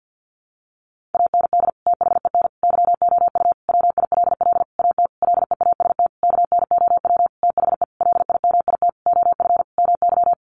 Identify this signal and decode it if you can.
Morse code, heard as a keyed tone, about 50 wpm, 700 Hz. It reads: AND THEN YOU WILL AT LEAST KNOW THE LIMIT OF MY
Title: Sentences from Top 1500 Words 50wpm